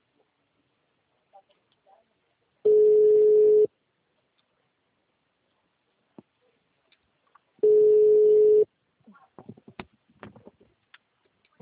Ringback tone (Europe, 425 Hz).